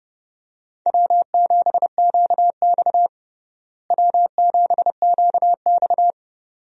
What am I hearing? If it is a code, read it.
Morse code, heard as a keyed tone, about 32 wpm, 700 Hz. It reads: W7QX W7QX
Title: USA Call Signs - 4 Characters - Repeated 2x - 30wpm